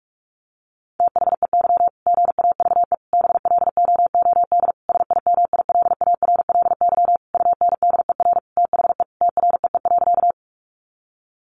Morse code, heard as a keyed tone, about 44 wpm, 700 Hz. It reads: THEY GAVE BLOOD SIMILARLY UNDER THE TREE.
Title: Sentences from Top 1400 Words 45wpm